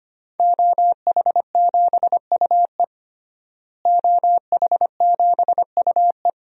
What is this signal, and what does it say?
Morse code, heard as a keyed tone, about 25 wpm, 700 Hz. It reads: OH7UE OH7UE
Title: DXCC-100 Call Signs - 5 Characters - Repeated 3x - 25wpm